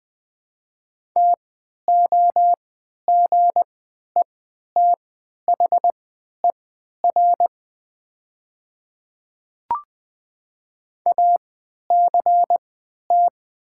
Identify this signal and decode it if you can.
Morse code, heard as a keyed tone, about 20 wpm, spaced as for 11 wpm, 700 Hz. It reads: TOGETHER ACT
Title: 3x Character Spacing - Top 500 Words - 20wpm